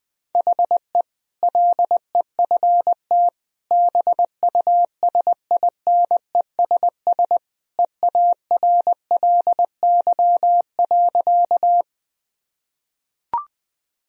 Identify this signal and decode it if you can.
Morse code, heard as a keyed tone, about 20 wpm, 700 Hz. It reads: HE LEFT BUSINESS EARLY.
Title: Sentences from Top 1000 Words - Review Part 2 - 20wpm